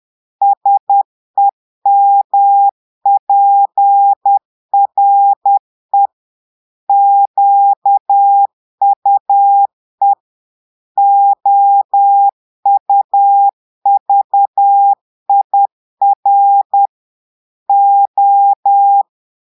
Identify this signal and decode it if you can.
Morse code, heard as a keyed tone, about 10 wpm, 800 Hz. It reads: SEMPRE QUE OUVIR O
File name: portugais texte en clair 2 vitesse 10 mots minutes